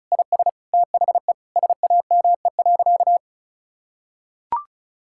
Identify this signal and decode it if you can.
Morse code, heard as a keyed tone, about 36 wpm, 700 Hz. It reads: IS THE SAME.
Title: Sentences from Top 1000 Words - Review Part 6 - 35wpm